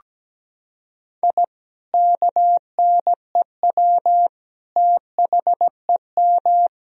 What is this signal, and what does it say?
Morse code, heard as a keyed tone, about 17 wpm, 700 Hz. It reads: I KNEW THEM
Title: Sentences from Top 1000 Words - Review Part 1 - 17wpm